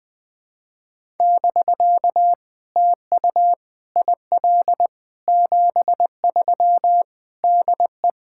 Morse code, heard as a keyed tone, about 20 wpm, 700 Hz. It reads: <BK> TU IL 73 DE
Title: POTA QSO Practice - 20wpm